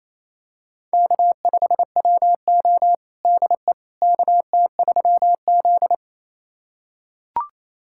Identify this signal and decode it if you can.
Morse code, heard as a keyed tone, about 27 wpm, 700 Hz. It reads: K5WO DE KT3Z
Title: USA Callsign DE Callsign - 4 Characters - 28wpm